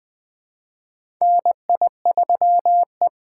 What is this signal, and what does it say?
Morse code, heard as a keyed tone, about 20 wpm, 700 Hz. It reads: NI3E
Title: USA Call Signs - 3x Character Spacing - 20wpm